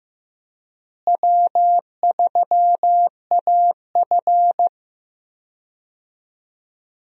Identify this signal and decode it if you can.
Morse code, heard as a keyed tone, about 15 wpm, 700 Hz. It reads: W3AF
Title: USA Call Signs - 4 Characters - Repeated 2x - 15wpm